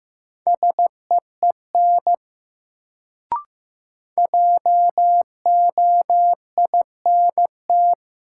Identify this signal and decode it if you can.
Morse code, heard as a keyed tone, about 15 wpm, 700 Hz. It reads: SEEN JOINT
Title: Sets of 5 Words Encore - New Word Drill - 15wpm